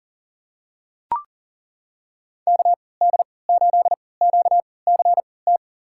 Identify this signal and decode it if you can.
Morse code, heard as a keyed tone, about 40 wpm, spaced as for 23 wpm, 700 Hz. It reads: KD8QCT
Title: USA Call Signs - 6 Characters - 3x Character Spacing - 40wpm